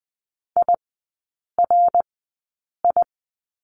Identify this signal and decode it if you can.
Morse code, heard as a keyed tone, about 20 wpm, spaced as for 8 wpm, 700 Hz. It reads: IRI
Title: 053 Lesson 10e - Letter r - Three Characters 20wpm